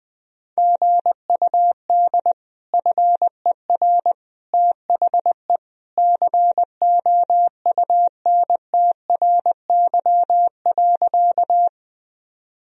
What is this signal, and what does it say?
Morse code, heard as a keyed tone, about 20 wpm, 700 Hz. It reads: GUD FER THE COUNTRY.
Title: Sentences from Top 300 Words and Abbreviations 20wpm